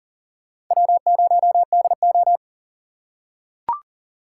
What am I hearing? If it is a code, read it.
Morse code, heard as a keyed tone, about 40 wpm, 700 Hz. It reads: W0DO
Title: USA Call Signs - 4 Characters - 4x Character Spacing - 40wpm